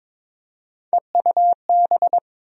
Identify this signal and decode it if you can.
Morse code, heard as a keyed tone, about 21 wpm, 700 Hz. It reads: EUB
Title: USA Call Sign Postfixes - 3 Letters 22wpm